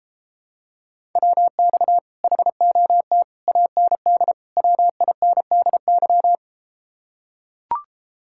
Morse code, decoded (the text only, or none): WX HOT AND WINDY